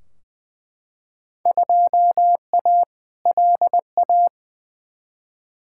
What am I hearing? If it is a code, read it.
Morse code, heard as a keyed tone, about 20 wpm, 700 Hz. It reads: 2A LA
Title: Field Day Exchanges Encore - 20wpm